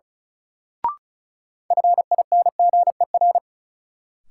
Morse code, heard as a keyed tone, about 36 wpm, 700 Hz. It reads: FINGER
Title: Sentences from Top 1600 Words - New Word Drill Alt - 35wpm